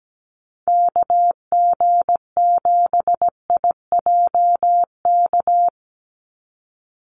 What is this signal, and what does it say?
Morse code, heard as a keyed tone, about 17 wpm, 700 Hz. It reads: KG7IJK
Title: USA Call Signs 17wpm